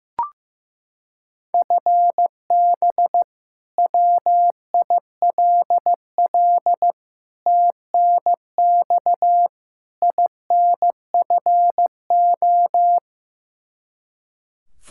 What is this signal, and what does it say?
Morse code, heard as a keyed tone, about 15 wpm, 700 Hz. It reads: FB WILL TNX INFO
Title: Practice QSO - Exchange 3 - More Info - 15wpm